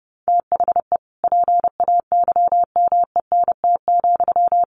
Morse code, heard as a keyed tone, about 30 wpm, 700 Hz. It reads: THE PAYMENT,
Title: Sentences from Top 1200 Words 30wpm